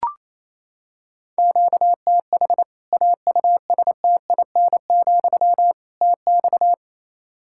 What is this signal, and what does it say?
Morse code, heard as a keyed tone, about 28 wpm, 700 Hz. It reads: QTH AUSTIN, TX
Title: Practice QSO - Exchange 1 - Beginning - 28wpm